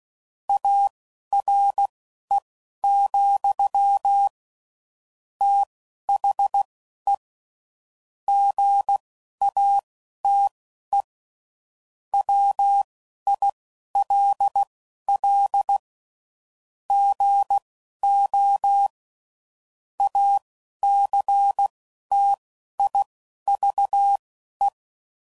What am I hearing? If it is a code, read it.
Morse code, heard as a keyed tone, about 16 wpm, spaced as for 11 wpm, 800 Hz. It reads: ARE, THE GATE WILL GO ACTIVE
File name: CW Code Groups @10WPM / Technical Subjects / OR Gate